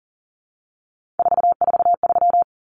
Morse code, heard as a keyed tone, about 40 wpm, 700 Hz. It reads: 443
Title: Numbers - 3-Digit 40wpm